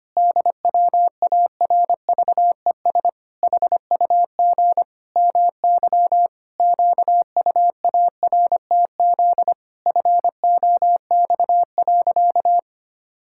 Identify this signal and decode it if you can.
Morse code, heard as a keyed tone, about 25 wpm, 700 Hz. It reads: DWARVES HUG MY QUARTZ FOX.
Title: Pangram Challenge v2 - 25wpm